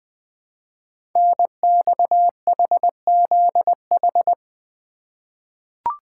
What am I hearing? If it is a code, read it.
Morse code, heard as a keyed tone, about 20 wpm, 700 Hz. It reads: NXHZH
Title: Code Groups - 5 Letters - 20wpm